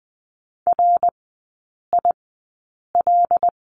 Morse code, heard as a keyed tone, about 20 wpm, spaced as for 8 wpm, 700 Hz. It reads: RIL
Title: 074 Lesson 13e - Letter L - Three Characters 20wpm